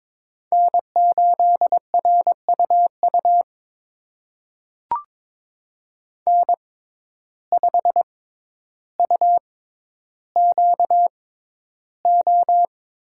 Morse code, heard as a keyed tone, about 23 wpm, spaced as for 11 wpm, 700 Hz. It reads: N8RUU N5UQO
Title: USA Call Signs - 5 Characters - 6x Character Spacing - 22wpm